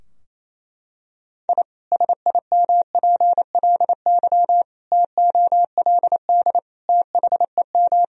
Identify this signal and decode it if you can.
Morse code, heard as a keyed tone, about 28 wpm, 700 Hz. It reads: I SIMPLY TOLD THEM
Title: Sentences from Top 1000 Words - Review Part 4 - 28wpm